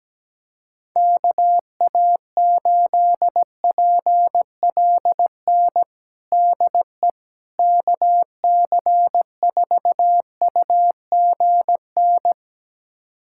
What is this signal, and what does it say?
Morse code, heard as a keyed tone, about 17 wpm, 700 Hz. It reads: KA8PLN DE KC4UGN
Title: USA Callsign DE Callsign 17wpm